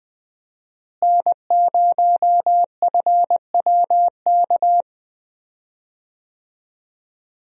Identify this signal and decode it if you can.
Morse code, heard as a keyed tone, about 20 wpm, 700 Hz. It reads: N0FWK
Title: USA Call Signs - 5 Characters - Repeated 2x - 20wpm